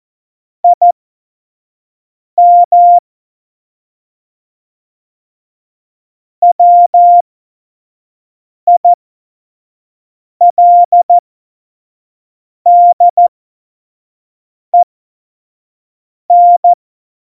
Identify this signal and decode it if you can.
Morse code, heard as a keyed tone, about 14 wpm, spaced as for 5 wpm, 700 Hz. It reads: IM WILDEN
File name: Morsekurs / Witze / Witz 07 Juli 2020, 25 F70